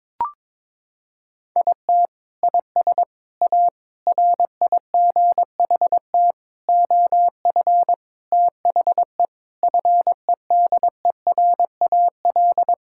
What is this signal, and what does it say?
Morse code, heard as a keyed tone, about 22 wpm, 700 Hz. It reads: IT IS A RIGHT OF THE FEDERAL
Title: Sentences from Top 1000 Words - Review Part 5 - 22wpm